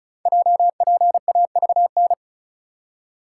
Morse code, heard as a keyed tone, about 33 wpm, 690 Hz. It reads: JPAVN
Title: Code Groups - 5 Letters - 35wpm